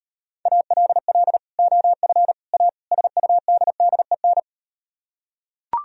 Morse code, heard as a keyed tone, about 38 wpm, 700 Hz. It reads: ALL OF A SUDDEN